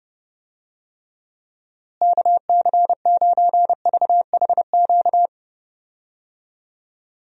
Morse code, heard as a keyed tone, about 31 wpm, 700 Hz. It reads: KC9VHQ
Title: USA Call Signs - 6 Characters - Repeated 2x - 30wpm